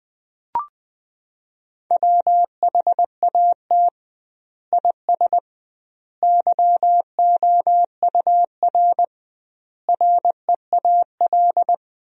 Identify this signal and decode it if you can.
Morse code, heard as a keyed tone, about 20 wpm, 700 Hz. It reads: WHAT IS YOUR REAL
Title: 2x Word-Spacing - Sentences from Top 300 Words - 20wpm